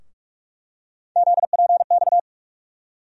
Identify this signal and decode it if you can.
Morse code, heard as a keyed tone, about 44 wpm, 700 Hz. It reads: ZPX
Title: USA Call Sign Postfixes - 3 Letters 45wpm